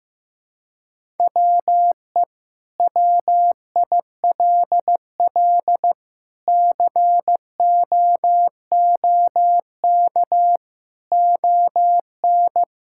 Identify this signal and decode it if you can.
Morse code, heard as a keyed tone, about 15 wpm, 700 Hz. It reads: WE WILL COOK ON
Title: Sentences from Top 1000 Words - Review Part 6 - 15wpm